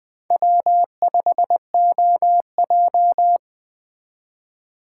Morse code, heard as a keyed tone, about 20 wpm, 700 Hz. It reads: W5OJ
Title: Speed Racing - USA 1x2 Call Signs 17wpm